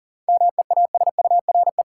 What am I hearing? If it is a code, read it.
Morse code, heard as a keyed tone, about 40 wpm, 700 Hz. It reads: MEASURE
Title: Sentences from Top 1000 Words - Review Part 1 - New Word Drill - 40wpm